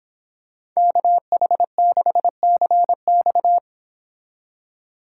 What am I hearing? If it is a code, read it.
Morse code, heard as a keyed tone, about 26 wpm, 700 Hz. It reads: KH6CX